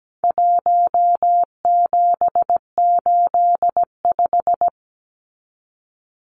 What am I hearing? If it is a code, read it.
Morse code, heard as a keyed tone, about 17 wpm, 700 Hz. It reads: 1785
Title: Numbers - 4-Digit 17wpm